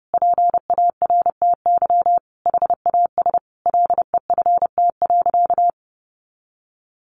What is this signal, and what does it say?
Morse code, heard as a keyed tone, about 28 wpm, 700 Hz. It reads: PARTY HAS LEFT.
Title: Sentences from Top 1500 Words 30wpm